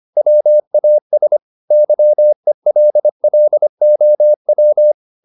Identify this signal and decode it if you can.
Morse code, heard as a keyed tone, about 25 wpm, 600 Hz. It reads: WAS YELLOW
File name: the old man and the sea part 1 25wpm